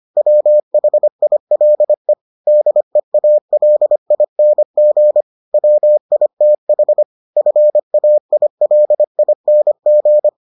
Morse code, heard as a keyed tone, about 25 wpm, 600 Hz. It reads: WHILE DEALING WITH FAILING